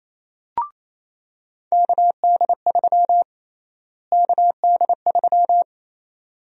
Morse code, heard as a keyed tone, about 28 wpm, 700 Hz. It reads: KD3 KD3
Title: USA Call Sign Prefixes - Repeated 2x - 28wpm